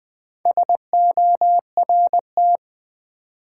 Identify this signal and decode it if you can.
Morse code, heard as a keyed tone, about 20 wpm, 700 Hz. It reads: SORT
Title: Sentences from Top 1300 Words - New Word Drill Alt - 20wpm